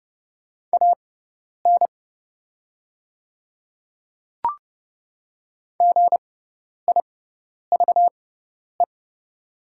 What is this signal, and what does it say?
Morse code, heard as a keyed tone, about 29 wpm, spaced as for 10 wpm, 700 Hz. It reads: AN GIVE
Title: 6x Character Spacing - Top 200 Words - 30wpm